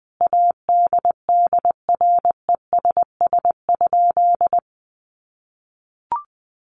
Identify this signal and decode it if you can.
Morse code, heard as a keyed tone, about 20 wpm, 700 Hz. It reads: ADDRESS?
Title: Sentences from Top 1000 Words 20wpm